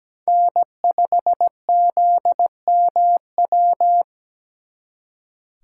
Morse code, heard as a keyed tone, about 20 wpm, 700 Hz. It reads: N5ZMW
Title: Speed Racing - USA 1x3 Call Signs 17wpm